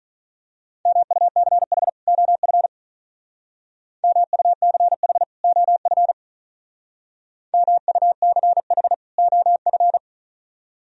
Morse code, heard as a keyed tone, about 42 wpm, 700 Hz. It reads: MUCH OF MUCH OF MUCH OF
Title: Speed-Racing - Sets of 2 Words 35wpm